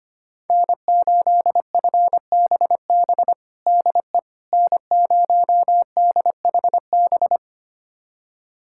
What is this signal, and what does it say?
Morse code, heard as a keyed tone, about 25 wpm, 700 Hz. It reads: N8FBB DE N0DHB
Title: USA Callsign DE Callsign - 5 Characters - 25wpm